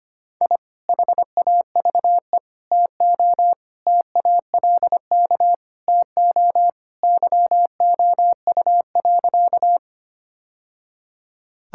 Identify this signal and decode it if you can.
Morse code, heard as a keyed tone, about 25 wpm, 700 Hz. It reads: I HAVE TO TALK TO YOU.
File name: practice / 25wpm / Sentences with 3-Word Phrases - 25wpm